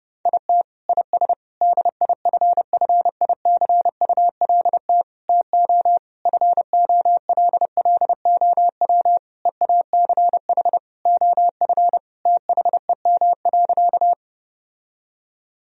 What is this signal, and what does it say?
Morse code, heard as a keyed tone, about 32 wpm, 700 Hz. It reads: IT IS DIFFICULT TO FOLLOW EACH OF THEM.
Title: Sentences from Top 1000 Words - Review Part 1 - 30wpm